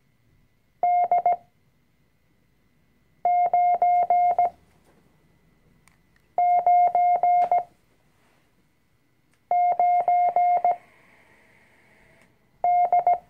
Morse code, heard as a keyed tone, about 17 wpm, spaced as for 4 wpm, 710 Hz. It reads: D999D
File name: CW-Online-Kurs / Anfänger / Lektion 005